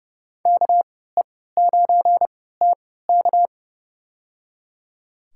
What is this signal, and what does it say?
Morse code, heard as a keyed tone, about 30 wpm, spaced as for 17 wpm, 700 Hz. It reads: KE9TK
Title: USA Call Signs - 5 Characters - 3x Character Spacing - 30wpm